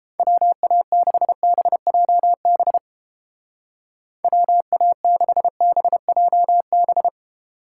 Morse code, heard as a keyed tone, about 30 wpm, 700 Hz. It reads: WA6BJB WA6BJB